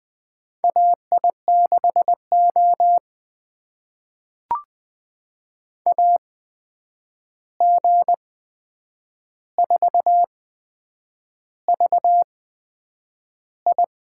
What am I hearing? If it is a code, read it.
Morse code, heard as a keyed tone, about 20 wpm, spaced as for 8 wpm, 700 Hz. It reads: AI6O AG4VI
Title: USA Call Signs - 8x Character Spacing - 20wpm